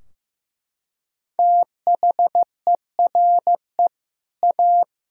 Morse code, heard as a keyed tone, about 15 wpm, 700 Hz. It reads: THERE A
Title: Sentences from Top 1000 Words - Review Part 3 - 15wpm